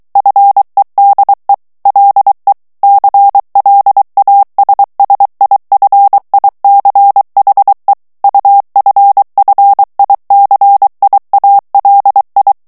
Morse code, heard as a keyed tone, about 23 wpm, 800 Hz. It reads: FEDE LE CLASSIFICHE UFFICIALI
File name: ITALIEN texte clair 2 vitesse 22 mots minute